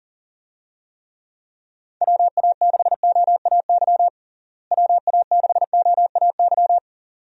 Morse code, heard as a keyed tone, about 40 wpm, 700 Hz. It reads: WA6OAY WA6OAY
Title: USA Call Signs - 6 Characters - Repeated 2x - 40wpm